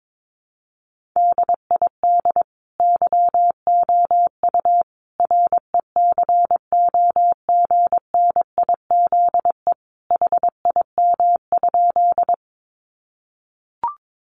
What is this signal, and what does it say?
Morse code, heard as a keyed tone, about 22 wpm, 700 Hz. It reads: DID YOU RECOGNIZE HIM?
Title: Sentences from Top 1400 Words 22wpm